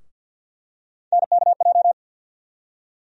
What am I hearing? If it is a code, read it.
Morse code, heard as a keyed tone, about 52 wpm, 700 Hz. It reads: NKJ
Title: USA Call Sign Postfixes - 3 Letters 50wpm